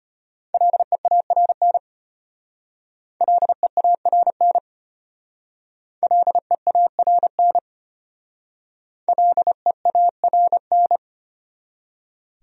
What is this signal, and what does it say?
Morse code, heard as a keyed tone, about 32 wpm, 700 Hz. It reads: LEARN LEARN LEARN LEARN